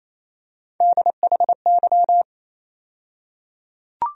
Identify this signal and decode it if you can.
Morse code, heard as a keyed tone, about 28 wpm, 700 Hz. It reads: DHY
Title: USA Call Sign Postfixes - 3 Letters 28wpm